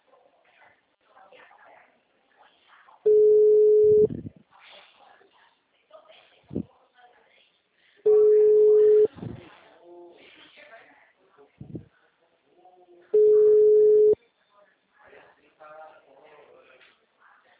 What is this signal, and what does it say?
Ringback tone (Europe, 425 Hz), with speech.